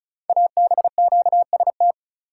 Morse code, heard as a keyed tone, about 36 wpm, 700 Hz. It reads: ABQST
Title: Code Groups - 5 Letters - 35wpm